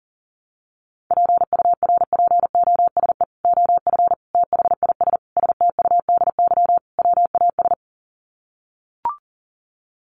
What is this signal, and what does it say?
Morse code, heard as a keyed tone, about 40 wpm, 700 Hz. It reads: PURPOSE OF THIS STUDY WAS
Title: Sets of 5 Words 40wpm